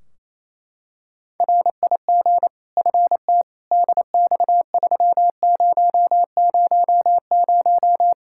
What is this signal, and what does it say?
Morse code, heard as a keyed tone, about 27 wpm, 700 Hz. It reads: RIG FT DX3000
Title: K1USN SST QSO Practice 28wpm